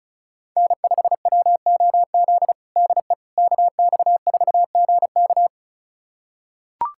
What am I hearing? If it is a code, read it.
Morse code, heard as a keyed tone, about 34 wpm, 700 Hz. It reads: N5WOZ DE KX4GK
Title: USA Callsign DE Callsign - 5 Characters - 35wpm